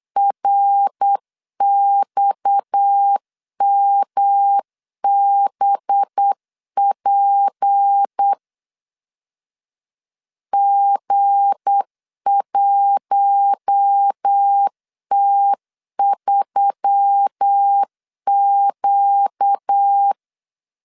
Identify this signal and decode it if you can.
Morse code, heard as a keyed tone, about 8 wpm, 800 Hz. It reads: RXMBP G1T3Q
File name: LEZIONI / 7 GF9Z / 7 GF 8wpm 3108